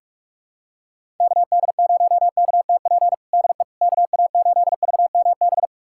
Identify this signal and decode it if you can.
Morse code, heard as a keyed tone, about 46 wpm, 700 Hz. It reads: KD0KTP DE KA8VMB
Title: USA Callsign DE Callsign 45wpm